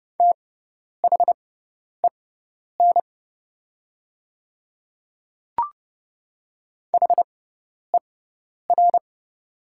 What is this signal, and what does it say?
Morse code, heard as a keyed tone, about 30 wpm, spaced as for 10 wpm, 700 Hz. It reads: THEN HER
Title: 6x Character Spacing - Top 100 Words - 30wpm